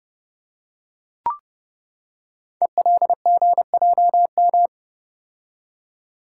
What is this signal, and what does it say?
Morse code, heard as a keyed tone, about 30 wpm, 700 Hz. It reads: ELGJM
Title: Code Groups - 5 Letters - 30wpm